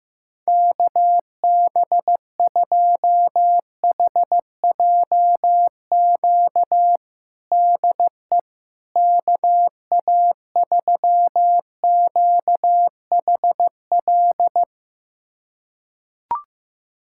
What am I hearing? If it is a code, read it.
Morse code, heard as a keyed tone, about 16 wpm, 700 Hz. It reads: KB2HJQ DE KA3QHL
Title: USA Callsign DE Callsign 15wpm